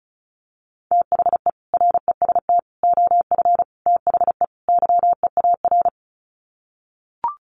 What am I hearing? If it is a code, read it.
Morse code, heard as a keyed tone, about 34 wpm, 700 Hz. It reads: THE REST OF THE YEAR
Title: Sets of 5 Words Encore 35wpm